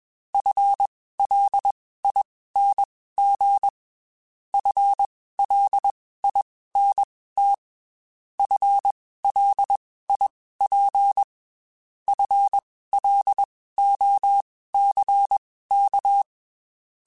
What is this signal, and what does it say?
Morse code, heard as a keyed tone, about 21 wpm, spaced as for 15 wpm, 800 Hz. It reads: FLING FLINT FLIP FLOCK